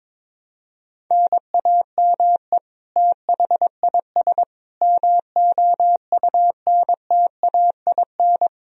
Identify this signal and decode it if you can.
Morse code, heard as a keyed tone, about 22 wpm, 700 Hz. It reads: NAME THIS MOUNTAIN
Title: Sentences from Top 1000 Words - Review Part 4 - 22wpm